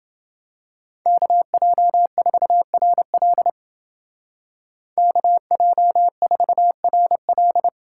Morse code, heard as a keyed tone, about 29 wpm, 700 Hz. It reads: KJ4RL KJ4RL